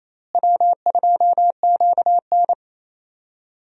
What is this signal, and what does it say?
Morse code, heard as a keyed tone, about 27 wpm, 700 Hz. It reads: W2QN
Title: USA Call Signs - 4 Characters - Repeated 3x - 28wpm